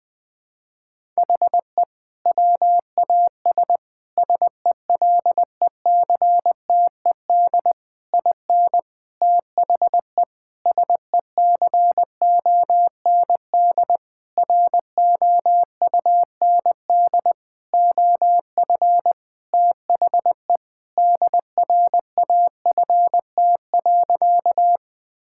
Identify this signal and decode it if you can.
Morse code, heard as a keyed tone, about 20 wpm, 700 Hz. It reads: HE WAS SELECTED IN THE SECOND ROUND OF THE DRAFT.
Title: Sentences with 6-Word Phrases - 20wpm